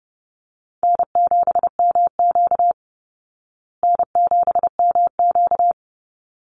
Morse code, heard as a keyed tone, about 30 wpm, 700 Hz. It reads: N7MQ N7MQ
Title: Call Signs Repeated 3x - 30wpm